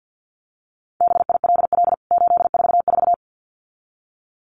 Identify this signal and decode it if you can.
Morse code, heard as a keyed tone, about 48 wpm, 700 Hz. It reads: BILL 844
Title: CWT Exchange 50wpm